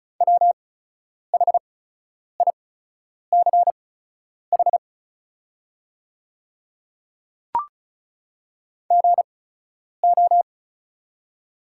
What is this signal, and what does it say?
Morse code, heard as a keyed tone, about 36 wpm, spaced as for 10 wpm, 700 Hz. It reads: WHICH GO
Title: 8x Character Spacing - Top 100 Words - 35wpm